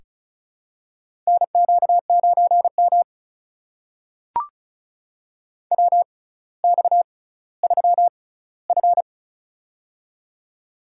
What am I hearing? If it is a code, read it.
Morse code, heard as a keyed tone, about 36 wpm, spaced as for 18 wpm, 700 Hz. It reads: NQ9M WX3F
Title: USA Call Signs - 4 Characters - 6x Character Spacing - 35wpm